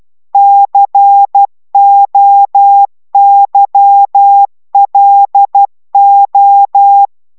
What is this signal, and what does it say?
Morse code, heard as a keyed tone, about 12 wpm, 800 Hz. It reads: COYLO